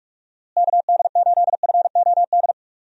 Morse code, heard as a keyed tone, about 47 wpm, 700 Hz. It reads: KD8FOD
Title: USA Call Signs - 6 Characters - Repeated 2x - 45wpm